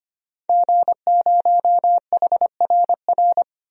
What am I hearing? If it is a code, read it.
Morse code, heard as a keyed tone, about 25 wpm, 700 Hz. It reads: G0HRR
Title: DXCC-100 Call Signs - 5 Characters - Repeated 3x - 25wpm